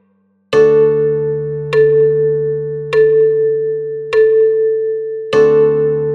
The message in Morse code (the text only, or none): none